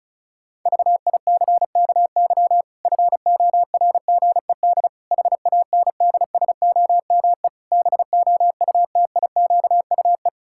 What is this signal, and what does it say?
Morse code, heard as a keyed tone, about 34 wpm, 700 Hz. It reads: VICKY FORGED HANDSOME BOUTIQUE
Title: Pangram Challenge v2 - 35wpm